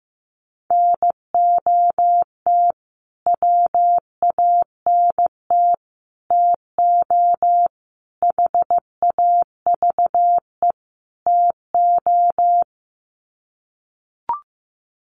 Morse code, heard as a keyed tone, about 15 wpm, 700 Hz. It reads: NOT WANT TO HAVE TO
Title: Sets of 5 Words Encore2 15wpm